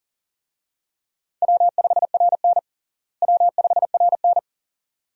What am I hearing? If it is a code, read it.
Morse code, heard as a keyed tone, about 37 wpm, 700 Hz. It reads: W5RN W5RN
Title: USA Call Signs - 4 Characters - Repeated 2x - 40wpm